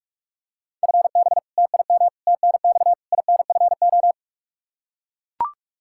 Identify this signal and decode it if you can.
Morse code, heard as a keyed tone, about 41 wpm, 700 Hz. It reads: FB TIM TNX INFO
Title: Practice QSO - Exchange 3 - More Info - 45wpm